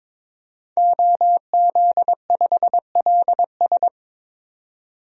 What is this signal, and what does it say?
Morse code, heard as a keyed tone, about 21 wpm, 700 Hz. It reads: OZ5LS
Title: DXCC-100 Call Signs - 5 Characters - 22wpm